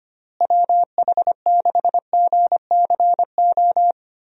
Morse code, heard as a keyed tone, about 25 wpm, 700 Hz. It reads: WH6GCO
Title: USA Call Signs - 6 Characters - Repeated 3x - 25wpm